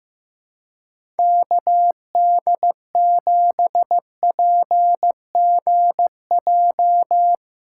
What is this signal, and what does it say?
Morse code, heard as a keyed tone, about 15 wpm, 700 Hz. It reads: KD7PGJ
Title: USA Callsign DE Callsign 15wpm